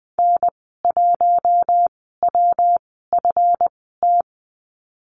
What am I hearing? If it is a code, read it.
Morse code, heard as a keyed tone, about 20 wpm, spaced as for 15 wpm, 700 Hz. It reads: N1WFT
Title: 179 Lesson 29g - Slash - Call Signs 20wpm